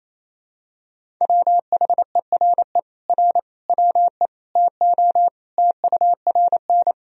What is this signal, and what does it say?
Morse code, heard as a keyed tone, about 28 wpm, 700 Hz. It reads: WHERE R WE TO TURN
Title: Sentences from Top 200 Words and Abbreviations 28wpm